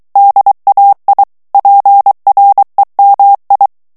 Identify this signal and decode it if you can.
Morse code, heard as a keyed tone, about 23 wpm, 800 Hz. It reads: DAI PREMI
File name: ITALIEN texte clair 2 vitesse 22 mots minute